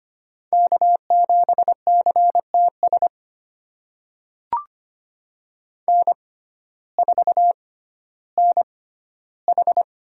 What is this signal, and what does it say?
Morse code, heard as a keyed tone, about 25 wpm, spaced as for 14 wpm, 700 Hz. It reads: K7CTS N4NH
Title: USA Call Signs - 6x Character Spacing - 25wpm